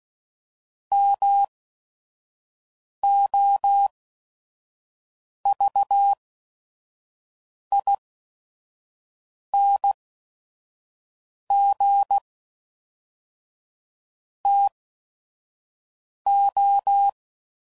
Morse code, heard as a keyed tone, about 16 wpm, spaced as for 5 wpm, 800 Hz. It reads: MOVING TO
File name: CW Code Groups @5WPM / QSOs / WD0FBT 1